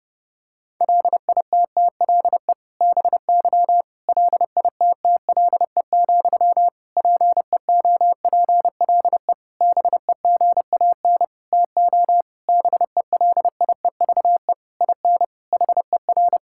Morse code, heard as a keyed tone, about 30 wpm, 700 Hz. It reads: LITTLE BY LITTLE, PEOPLE BEGAN TO BELIEVE IN HER
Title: Sentences from Top 1600 Words 30wpm